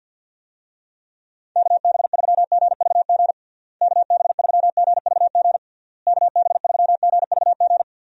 Morse code, heard as a keyed tone, about 50 wpm, 700 Hz. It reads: KB3GVG KB3GVG KB3GVG
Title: USA Call Signs - 6 Characters - Repeated 3x - 50wpm